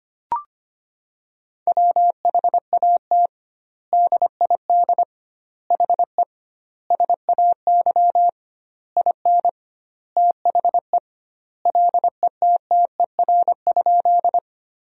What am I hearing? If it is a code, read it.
Morse code, heard as a keyed tone, about 25 wpm, 700 Hz. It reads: WHAT DID HE SAY IN THE LETTER?
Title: 2x Word-Spacing - Sentences from Top 300 Words - 25wpm